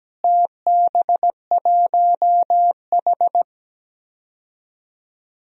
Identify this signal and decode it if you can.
Morse code, heard as a keyed tone, about 17 wpm, 700 Hz. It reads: TB1H
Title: DXCC-100 Call Signs - 4 Characters - Repeated 3x - 17wpm